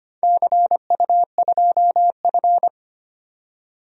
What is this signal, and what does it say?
Morse code, heard as a keyed tone, about 25 wpm, 700 Hz. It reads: CU2F
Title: DXCC-100 Call Signs - 4 Characters - 25wpm